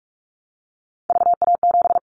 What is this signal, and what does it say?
Morse code, heard as a keyed tone, about 44 wpm, 700 Hz. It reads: VA7
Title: Callsign Prefixes for Top 25 DXCC - 45wpm